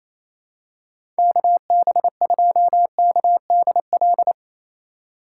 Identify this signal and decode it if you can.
Morse code, heard as a keyed tone, about 27 wpm, 700 Hz. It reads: KB2KDL
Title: USA Call Signs - 6 Characters - 4x Character Spacing - 28wpm